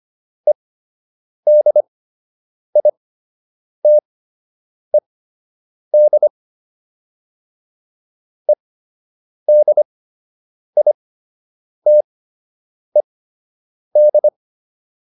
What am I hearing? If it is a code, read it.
Morse code, heard as a keyed tone, about 25 wpm, spaced as for 8 wpm, 600 Hz. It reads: EDITED EDITED
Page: fundamental-practice-files